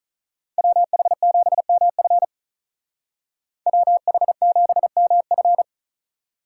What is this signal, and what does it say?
Morse code, heard as a keyed tone, about 43 wpm, 700 Hz. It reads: WH7MF WH7MF
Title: Speed Racing - USA 2x2 Call Signs 35wpm